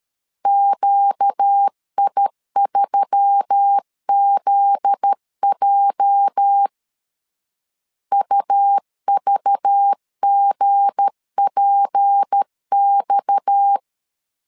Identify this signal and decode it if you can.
Morse code, heard as a keyed tone, about 13 wpm, 800 Hz. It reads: QI3ZJ UVGPX
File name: LEZIONI / 8 YJ45 / 8 YJ 12wpm 2235